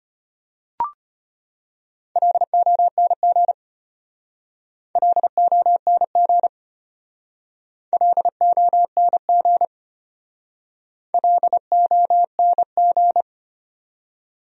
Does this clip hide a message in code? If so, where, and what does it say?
Yes: Morse code from 2.2 to 13.2 s, heard as a keyed tone, about 32 wpm, 700 Hz. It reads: LONG LONG LONG LONG